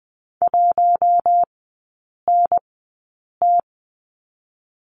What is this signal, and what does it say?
Morse code, heard as a keyed tone, about 20 wpm, spaced as for 8 wpm, 700 Hz. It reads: 1NT
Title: 060 Lesson 11e - Letter h - Three Characters 20wpm